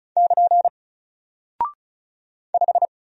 Morse code, heard as a keyed tone, about 34 wpm, 700 Hz. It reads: <KN> 5
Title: Single Character (Letter:Number:Punctuation:Prosign) Alt 35wpm